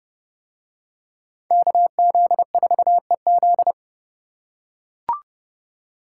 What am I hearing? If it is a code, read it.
Morse code, heard as a keyed tone, about 31 wpm, 700 Hz. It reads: KZ4EZ